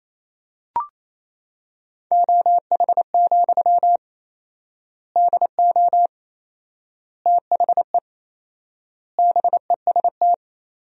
Morse code, heard as a keyed tone, about 28 wpm, 700 Hz. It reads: OH, DO THE BEST
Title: 4x Word-Spacing - Sentences from Top 500 Words - 28wpm